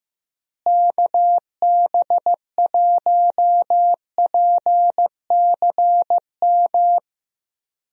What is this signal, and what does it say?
Morse code, heard as a keyed tone, about 15 wpm, 700 Hz. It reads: KB1PCM
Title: USA Call Signs - 6 Characters - Repeated 3x - 15wpm